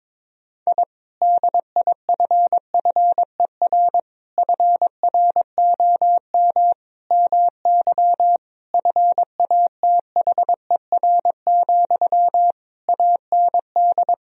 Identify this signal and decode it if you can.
Morse code, heard as a keyed tone, about 22 wpm, 700 Hz. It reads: I DIFFER FROM MY FATHER, AND
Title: Sentences from Top 1000 Words - Review Part 1 - 22wpm